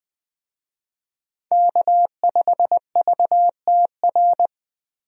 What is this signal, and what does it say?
Morse code, heard as a keyed tone, about 21 wpm, 700 Hz. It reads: K5VTR
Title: USA Call Signs - 5 Characters - Repeated 2x - 20wpm